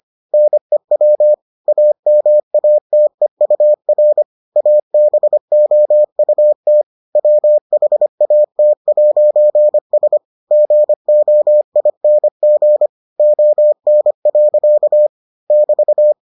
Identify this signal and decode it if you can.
Morse code, heard as a keyed tone, about 25 wpm, 600 Hz. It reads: NEW AMATEUR ABOUT WHAT'S GOING ON. <BT>